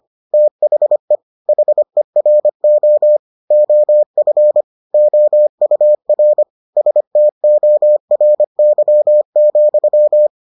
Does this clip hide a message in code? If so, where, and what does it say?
Yes: Morse code from 0.3 to 10.3 s, heard as a keyed tone, about 25 wpm, 600 Hz. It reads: THE HERO OF OUR STORY,